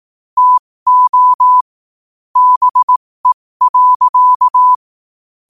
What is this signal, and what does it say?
Morse code, heard as a keyed tone, about 18 wpm, spaced as for 15 wpm, 1000 Hz. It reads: TO BE.
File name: qotdfiles / 15wpm